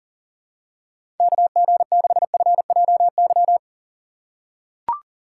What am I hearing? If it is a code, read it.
Morse code, heard as a keyed tone, about 40 wpm, 700 Hz. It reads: KG6FJY
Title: USA Call Signs - 6 Characters - 6x Character Spacing - 40wpm